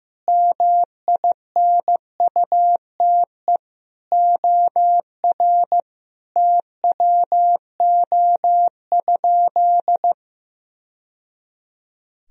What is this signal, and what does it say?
Morse code, heard as a keyed tone, about 15 wpm, 700 Hz. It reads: MINUTE OR TWO?
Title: Sentences from Top 1000 Words - Review Part 1 - 15wpm